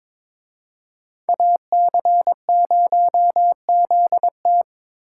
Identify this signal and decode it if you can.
Morse code, heard as a keyed tone, about 26 wpm, 700 Hz. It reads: AC0ZT
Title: Speed Racing - USA 2x2 Call Signs 22wpm